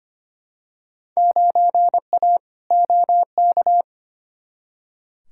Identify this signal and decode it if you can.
Morse code, heard as a keyed tone, about 25 wpm, 700 Hz. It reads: 9A OK
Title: Field Day Exchanges Encore - 25wpm